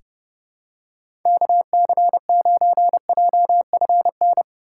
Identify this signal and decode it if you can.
Morse code, heard as a keyed tone, about 30 wpm, 700 Hz. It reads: KC9JFN
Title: USA Call Signs - 6 Characters - 3x Character Spacing - 30wpm